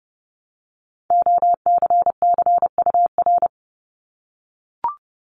Morse code, heard as a keyed tone, about 30 wpm, 700 Hz. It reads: OCCUR
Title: Multi-letter Phonemes 30wpm